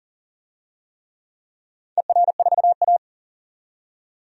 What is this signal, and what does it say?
Morse code, heard as a keyed tone, about 42 wpm, 700 Hz. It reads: ER4A
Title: DXCC-100 Call Signs - 4 Characters - 40wpm